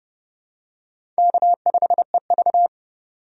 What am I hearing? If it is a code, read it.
Morse code, heard as a keyed tone, about 31 wpm, 700 Hz. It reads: K5EV
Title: USA Call Signs - 4 Characters - 30wpm